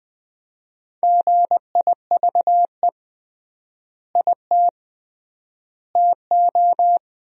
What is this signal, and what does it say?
Morse code, heard as a keyed tone, about 21 wpm, 700 Hz. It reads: GIVE IT TO
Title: 3x Word-Spacing - Sentences from Top 200 Words - 20wpm